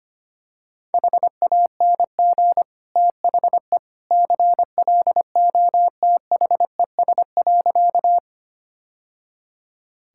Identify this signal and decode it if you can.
Morse code, heard as a keyed tone, about 25 wpm, 700 Hz. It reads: HANG THE CLOTHES.
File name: Easy Sentences with Hints - 25wpm